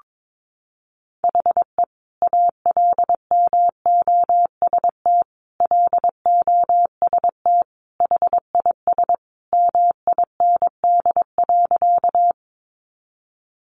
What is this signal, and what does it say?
Morse code, heard as a keyed tone, about 22 wpm, 700 Hz. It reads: HE ALMOST LOST HIS MIND.
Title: Sentences from Top 1400 Words 22wpm